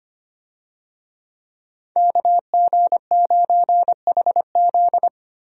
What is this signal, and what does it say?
Morse code, heard as a keyed tone, about 25 wpm, 700 Hz. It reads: KG9HZ
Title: USA Call Signs - 5 Characters - 25wpm